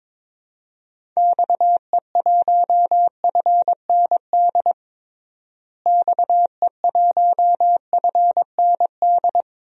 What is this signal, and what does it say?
Morse code, heard as a keyed tone, about 21 wpm, 700 Hz. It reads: XE1FND XE1FND
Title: DXCC-100 Call Signs - 6 Characters - Repeated 3x - 22wpm